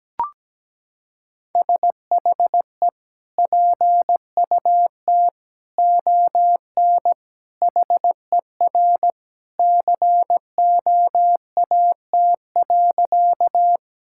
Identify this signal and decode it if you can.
Morse code, heard as a keyed tone, about 17 wpm, 700 Hz. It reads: SHE PUT ON HER COAT.
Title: Sentences from Top 1600 Words 17wpm